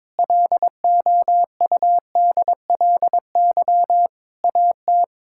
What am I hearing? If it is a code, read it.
Morse code, heard as a keyed tone, about 22 wpm, 700 Hz. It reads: LOUDLY AT
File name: Difficult Sentences with Hints - 22wpm